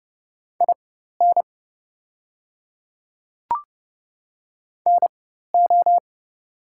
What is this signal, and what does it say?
Morse code, heard as a keyed tone, about 31 wpm, spaced as for 14 wpm, 690 Hz. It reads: IN NO
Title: 4x Character Spacing - Top 100 Words - 30wpm